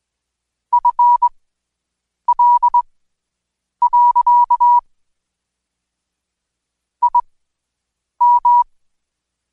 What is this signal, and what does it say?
Morse code, heard as a keyed tone, about 21 wpm, spaced as for 7 wpm, 970 Hz. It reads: FL. IM
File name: Code Quick / AudioDownld2 / Seg113-Day18-DayDrill 18